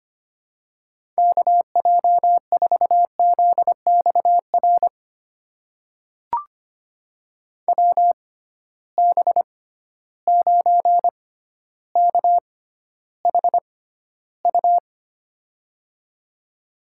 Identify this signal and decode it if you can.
Morse code, heard as a keyed tone, about 25 wpm, spaced as for 13 wpm, 700 Hz. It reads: KJ4ZXR WB9KHU
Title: USA Call Signs - 6 Characters - 6x Character Spacing - 25wpm